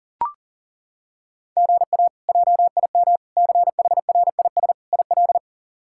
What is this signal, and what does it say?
Morse code, heard as a keyed tone, about 40 wpm, 700 Hz. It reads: GA JIM CHRIS IL
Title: K1USN SST QSO Practice - 40wpm